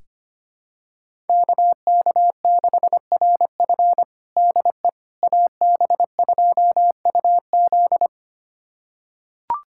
Morse code, heard as a keyed tone, about 25 wpm, 700 Hz. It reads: KK6RF DE AB2UZ
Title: USA Callsign DE Callsign - 5 Characters - 25wpm